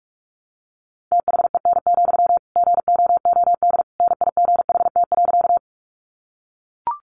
Morse code, heard as a keyed tone, about 44 wpm, 700 Hz. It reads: THEN, GOOD NIGHT.
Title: Sentences from Top 300 Words 45wpm